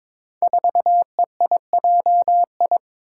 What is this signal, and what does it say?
Morse code, heard as a keyed tone, about 22 wpm, 700 Hz. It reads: 4EIJI
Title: Five Challenging Letter-Number Combinations 22wpm